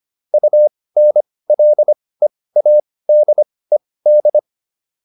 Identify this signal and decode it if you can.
Morse code, heard as a keyed tone, about 25 wpm, spaced as for 18 wpm, 600 Hz. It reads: UNLEADED